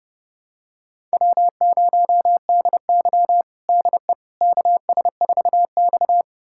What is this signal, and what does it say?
Morse code, heard as a keyed tone, about 32 wpm, 700 Hz. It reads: W0DY DE KS4X
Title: USA Callsign DE Callsign - 4 Characters - 30wpm